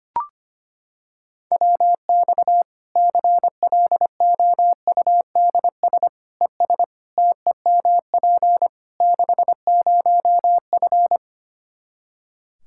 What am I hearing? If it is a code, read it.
Morse code, heard as a keyed tone, about 25 wpm, 700 Hz. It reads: WX CLOUDS ES TEMP 60F
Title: Practice QSO - Exchange 2 - Info - 25wpm